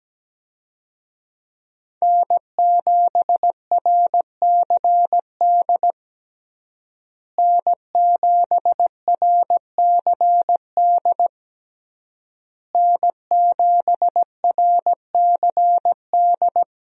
Morse code, heard as a keyed tone, about 17 wpm, 700 Hz. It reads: N7RCD N7RCD N7RCD
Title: USA Call Signs - 5 Characters - Repeated 3x - 17wpm